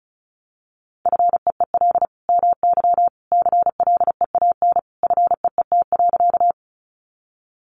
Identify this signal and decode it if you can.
Morse code, heard as a keyed tone, about 34 wpm, 700 Hz. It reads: FEEL MY CLEAN FEET.
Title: Sentences from Top 1100 Words 35wpm